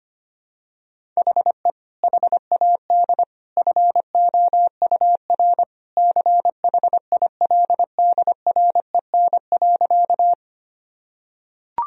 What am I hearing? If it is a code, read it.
Morse code, heard as a keyed tone, about 25 wpm, 700 Hz. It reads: HE HAD FOUR CHILDREN.
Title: Sentences from Top 1000 Words - Review Part 1 - 25wpm